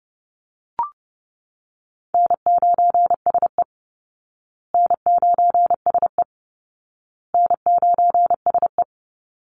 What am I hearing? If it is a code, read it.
Morse code, heard as a keyed tone, about 30 wpm, 700 Hz. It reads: N9SE N9SE N9SE
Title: Easy Call Signs Repeated 3x - 30wpm